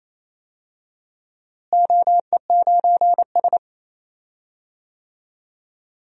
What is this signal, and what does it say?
Morse code, heard as a keyed tone, about 28 wpm, 700 Hz. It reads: OE9S
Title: DXCC-100 Call Signs - 4 Characters - Repeated 2x - 28wpm